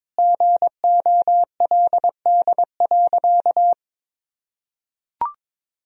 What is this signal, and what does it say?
Morse code, heard as a keyed tone, about 22 wpm, 700 Hz. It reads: GOLD.
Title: Sentences from Top 1000 Words - Review Part 3 - 22wpm